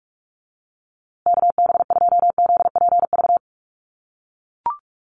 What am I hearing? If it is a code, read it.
Morse code, heard as a keyed tone, about 44 wpm, 700 Hz. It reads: KB2ZPV
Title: USA Call Signs Repeated 2x 45wpm